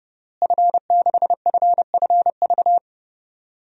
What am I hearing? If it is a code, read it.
Morse code, heard as a keyed tone, about 30 wpm, 700 Hz. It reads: F6FFV
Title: DXCC-100 Call Signs - 5 Characters - Repeated 2x - 30wpm